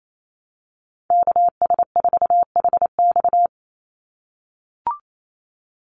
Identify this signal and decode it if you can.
Morse code, heard as a keyed tone, about 28 wpm, 700 Hz. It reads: KS4HX
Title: USA Call Signs 28wpm